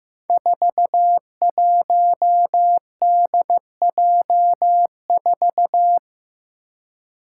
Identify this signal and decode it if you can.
Morse code, heard as a keyed tone, about 16 wpm, 700 Hz. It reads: 41DJ4
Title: Five Challenging Letter-Number Combinations 15wpm